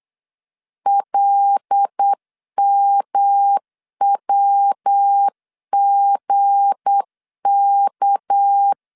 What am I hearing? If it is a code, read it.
Morse code, heard as a keyed tone, about 8 wpm, 800 Hz. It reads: LMWGK
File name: LEZIONI / 8 YJ45 / 8 Y 8wpm 3245